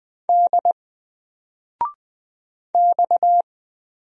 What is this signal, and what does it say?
Morse code, heard as a keyed tone, about 20 wpm, 700 Hz. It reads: D X
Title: Single Letter-Number 20wpm